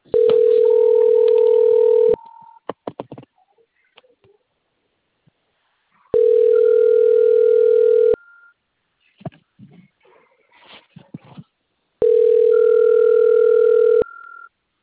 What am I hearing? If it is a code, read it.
Ringback tone (North America), 440 + 480 Hz.